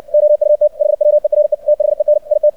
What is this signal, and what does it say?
Morse code, heard as a keyed tone, about 64 wpm, 600 Hz. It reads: OUT HWERE THEN IT
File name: archived / CRSnet